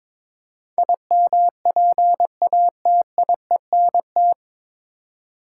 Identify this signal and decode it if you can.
Morse code, heard as a keyed tone, about 22 wpm, 700 Hz. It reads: IMPATIENT
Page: Morse Code Ninja practice 22wpm